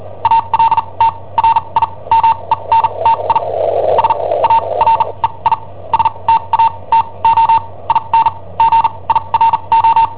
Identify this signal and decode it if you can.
Morse code, heard as a keyed tone, about 43 wpm, spaced as for 28 wpm, 910 Hz. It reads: ALTRIMENTI SAREI STATO IN GIRO
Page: Cassetta esercizi preparata da I7OHP per IW5ELP